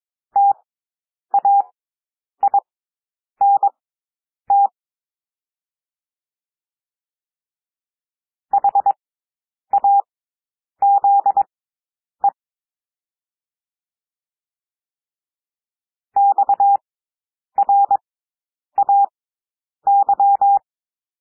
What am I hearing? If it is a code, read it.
Morse code, heard as a keyed tone, about 22 wpm, spaced as for 9 wpm, 800 Hz. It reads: TAINT HAZE XRAY
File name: cwmp3 / W2 SOUND WORDS 22 wpm